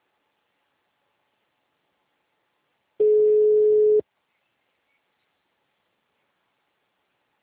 Ringback tone (Europe, 425 Hz).
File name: out- -2040- - - .206